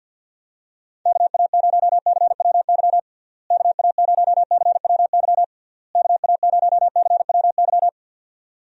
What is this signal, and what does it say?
Morse code, heard as a keyed tone, about 46 wpm, 700 Hz. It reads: KA0CWY KA0CWY KA0CWY
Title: USA Call Signs - 6 Characters - Repeated 3x - 50wpm